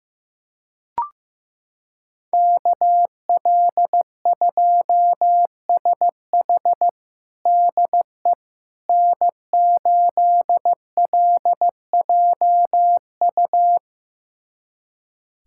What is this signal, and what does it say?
Morse code, heard as a keyed tone, about 15 wpm, 700 Hz. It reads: KL2SH DE N8LJU
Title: USA Callsign DE Callsign 15wpm